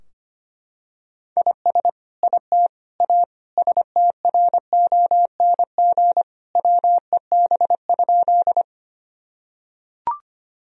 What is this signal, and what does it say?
Morse code, heard as a keyed tone, about 25 wpm, 700 Hz. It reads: IS IT A STRONG WEB?
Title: Sentences from Top 1000 Words - Review Part 4 - 25wpm